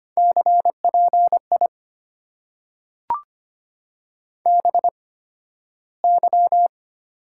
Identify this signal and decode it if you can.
Morse code, heard as a keyed tone, about 25 wpm, spaced as for 13 wpm, 700 Hz. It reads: CPI BY
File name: Top 90 QSO Elements - 8x Character Spacing - 25wpm